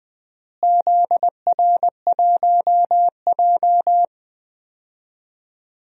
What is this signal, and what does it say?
Morse code, heard as a keyed tone, about 20 wpm, 700 Hz. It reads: ZR1J
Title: DXCC-100 Call Signs - 4 Characters - Repeated 2x - 20wpm